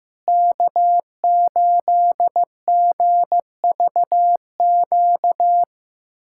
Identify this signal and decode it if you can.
Morse code, heard as a keyed tone, about 15 wpm, 700 Hz. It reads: K8GVQ
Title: USA Callsign DE Callsign - 5 Characters - 15wpm